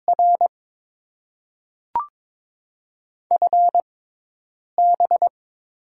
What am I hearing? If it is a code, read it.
Morse code, heard as a keyed tone, about 21 wpm, spaced as for 8 wpm, 700 Hz. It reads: R FB
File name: Top 20 QSO Elements - 6x Character Spacing - 22wpm